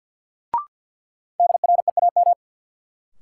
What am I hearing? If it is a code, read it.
Morse code, heard as a keyed tone, about 53 wpm, 700 Hz. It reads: DREAM
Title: Sentences from Top 1500 Words - New Word Drill Alt - 50wpm